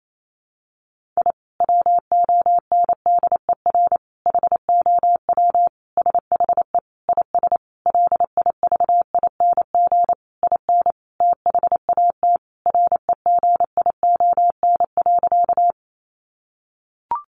Morse code, heard as a keyed tone, about 28 wpm, 700 Hz. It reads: I WONDER HOW SHE IS LIVING IN THAT REGION.
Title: Sentences from Top 1500 Words 28wpm